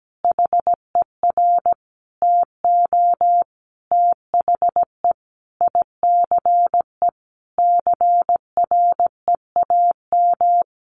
Morse code, heard as a keyed tone, about 17 wpm, 700 Hz. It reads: HER TO THE ICE CREAM
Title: Additional Easy Phrases 17wpm